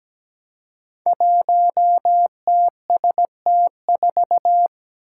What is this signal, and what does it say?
Morse code, heard as a keyed tone, about 17 wpm, 700 Hz. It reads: 1TST4
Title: Five Challenging Letter-Number Combinations 17wpm